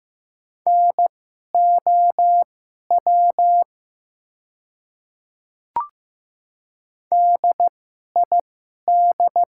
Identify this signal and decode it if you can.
Morse code, heard as a keyed tone, about 15 wpm, spaced as for 11 wpm, 690 Hz. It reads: NOW DID
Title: 2x Character Spacing - Top 100 Words - 15wpm